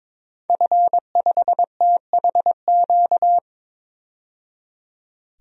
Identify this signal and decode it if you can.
Morse code, heard as a keyed tone, about 21 wpm, 700 Hz. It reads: F5THQ
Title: DXCC-100 Call Signs - 5 Characters - 22wpm